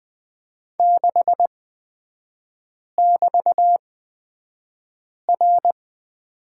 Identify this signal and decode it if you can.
Morse code, heard as a keyed tone, about 20 wpm, spaced as for 5 wpm, 700 Hz. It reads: 6<BT>R
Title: 224 Lesson 36e - BT - Three Characters 20wpm